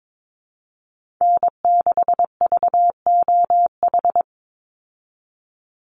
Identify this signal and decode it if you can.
Morse code, heard as a keyed tone, about 22 wpm, 700 Hz. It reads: N6VOH
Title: Call Signs Repeated 3x - 22wpm